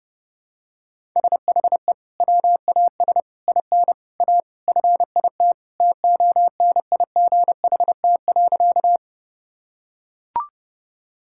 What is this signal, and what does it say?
Morse code, heard as a keyed tone, about 31 wpm, 700 Hz. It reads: SHE WAS IN A FIT TONIGHT.
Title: Sentences from Top 1000 Words - Review Part 5 - 30wpm